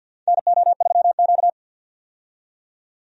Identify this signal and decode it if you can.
Morse code, heard as a keyed tone, about 50 wpm, 700 Hz. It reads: NO3Q
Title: USA Call Signs - 4 Characters - 50wpm